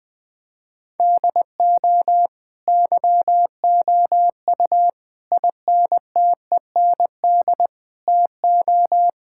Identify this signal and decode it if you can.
Morse code, heard as a keyed tone, about 20 wpm, 700 Hz. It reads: DO YOU INTEND TO
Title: Sentences from Top 1600 Words 20wpm